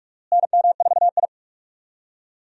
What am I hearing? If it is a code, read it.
Morse code, heard as a keyed tone, about 43 wpm, 700 Hz. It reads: NM4I
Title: USA Call Signs - 4 Characters - Repeated 3x - 45wpm